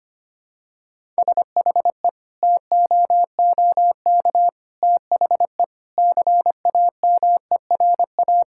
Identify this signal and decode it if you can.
Morse code, heard as a keyed tone, about 25 wpm, 700 Hz. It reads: SHE TOOK THE CAMERA
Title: Sentences from Top 1000 Words - Review Part 5 - 25wpm